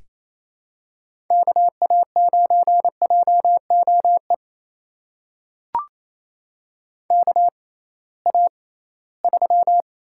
Morse code, heard as a keyed tone, about 28 wpm, spaced as for 18 wpm, 700 Hz. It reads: KA9JOE KA3
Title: USA Call Signs - 6 Characters - 6x Character Spacing - 28wpm